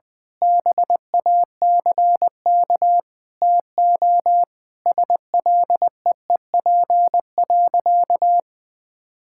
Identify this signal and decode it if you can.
Morse code, heard as a keyed tone, about 19 wpm, 700 Hz. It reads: BACK TO SLEEP.
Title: Sentences with 7-Word Phrases - 20wpm